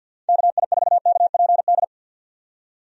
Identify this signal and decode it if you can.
Morse code, heard as a keyed tone, about 50 wpm, 700 Hz. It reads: KI4KPD
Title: USA Call Signs - 6 Characters - 50wpm